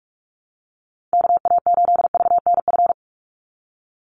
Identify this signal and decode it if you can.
Morse code, heard as a keyed tone, about 44 wpm, 700 Hz. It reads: KA8VNF
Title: USA Call Signs 45wpm